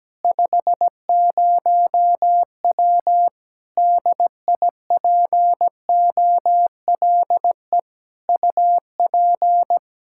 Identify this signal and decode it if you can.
Morse code, heard as a keyed tone, about 17 wpm, 700 Hz. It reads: 50W DIPOLE UP
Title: K1USN SST QSO Practice 17wpm